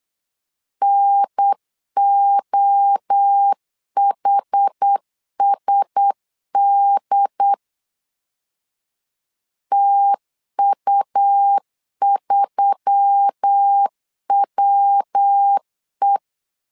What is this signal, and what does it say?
Morse code, heard as a keyed tone, about 8 wpm, 800 Hz. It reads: NOHSD TU3WE
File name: LEZIONI / 4 Q1S3 8wpm 3037